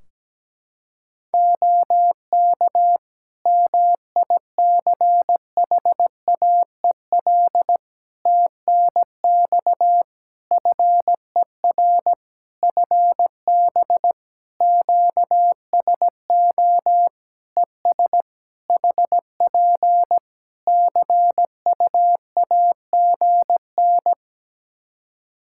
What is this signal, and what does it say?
Morse code, heard as a keyed tone, about 17 wpm, 700 Hz. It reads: OK MICHAEL TNX FER FB QSO ES HP CUAGN
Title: K1USN SST QSO Practice 17wpm